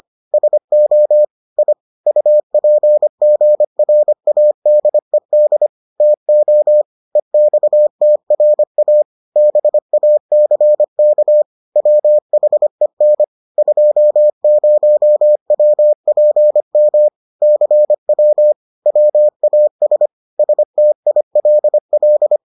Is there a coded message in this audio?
Yes: Morse code from 0.3 to 22.4 s, heard as a keyed tone, about 25 wpm, 600 Hz. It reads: SO I UPGRADED TO EXTRA BACK WHEN 20WPM CW WAS STILL